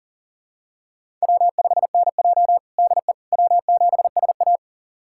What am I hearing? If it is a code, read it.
Morse code, heard as a keyed tone, about 40 wpm, 700 Hz. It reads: W5NJ DE W7SA
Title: USA Callsign DE Callsign - 4 Characters - 40wpm